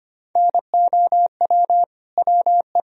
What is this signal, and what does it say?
Morse code, heard as a keyed tone, about 25 wpm, 700 Hz. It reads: NOW WE
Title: Sentences from Top 1000 Words - Review Part 1 - 25wpm